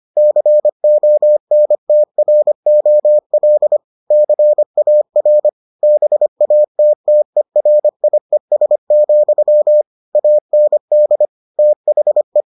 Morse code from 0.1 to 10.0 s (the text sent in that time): CONTROL CAR BATTERIES,